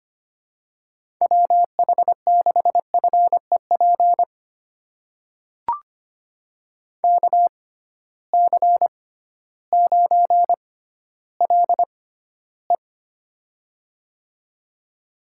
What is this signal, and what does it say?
Morse code, heard as a keyed tone, about 25 wpm, spaced as for 14 wpm, 700 Hz. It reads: WH6FEP KC9LE
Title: USA Call Signs - 6x Character Spacing - 25wpm